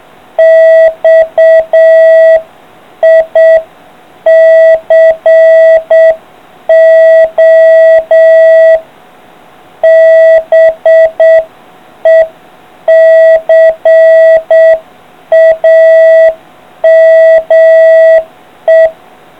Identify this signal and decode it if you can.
Morse code, heard as a keyed tone, about 7 wpm, 650 Hz. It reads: XICO BECAME